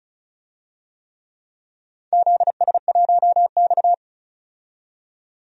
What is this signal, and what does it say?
Morse code, heard as a keyed tone, about 36 wpm, 700 Hz. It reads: ZS1X
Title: DXCC-100 Call Signs - 4 Characters - 35wpm